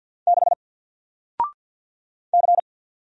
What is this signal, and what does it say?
Morse code, heard as a keyed tone, about 50 wpm, 700 Hz. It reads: 6 C
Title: Single Letter-Number Alt 50wpm